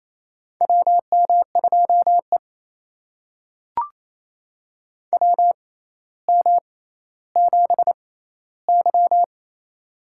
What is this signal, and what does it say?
Morse code, heard as a keyed tone, about 28 wpm, spaced as for 14 wpm, 700 Hz. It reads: WM2E WM7Y
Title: USA Call Signs - 4 Characters - 6x Character Spacing - 28wpm